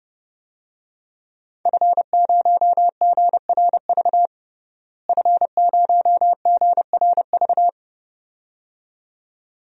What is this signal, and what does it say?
Morse code, heard as a keyed tone, about 30 wpm, 700 Hz. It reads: F0GRV F0GRV
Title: DXCC-100 Call Signs - 5 Characters - Repeated 2x - 30wpm